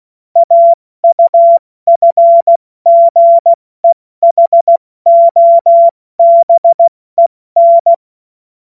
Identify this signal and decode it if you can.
Morse code, heard as a keyed tone, about 16 wpm, 680 Hz. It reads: AUFGEHOBEN
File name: Morsekurs / Witze / Witz 17. August 2021, 70 F80